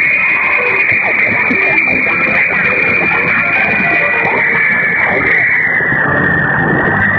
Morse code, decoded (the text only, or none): none